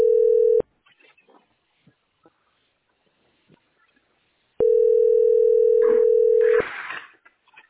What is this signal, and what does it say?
Ringback tone (North America), 440 + 480 Hz.